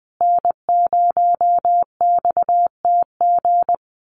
Morse code, heard as a keyed tone, about 20 wpm, 700 Hz. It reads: N0XTG
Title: USA Call Signs Repeated 3x 20wpm